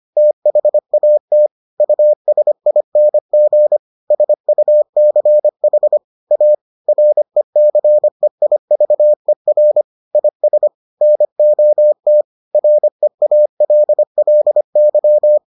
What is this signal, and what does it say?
Morse code, heard as a keyed tone, about 25 wpm, 600 Hz. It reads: THAT USING SUCH A RECEIVER IS NOT REALLY